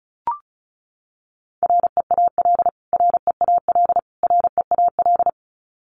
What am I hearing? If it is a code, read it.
Morse code, heard as a keyed tone, about 34 wpm, 700 Hz. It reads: REAL REAL REAL
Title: Top 500 Words Repeated 35wpm